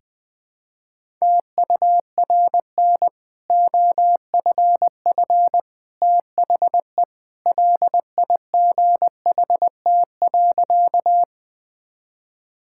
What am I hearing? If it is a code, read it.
Morse code, heard as a keyed tone, about 19 wpm, 700 Hz. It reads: TURN OFF THE LIGHT.
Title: Easy Sentences with Hints - 20wpm